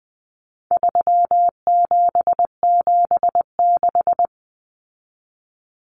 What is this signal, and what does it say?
Morse code, heard as a keyed tone, about 20 wpm, 700 Hz. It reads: 3776
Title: Numbers - 4-Digit 20wpm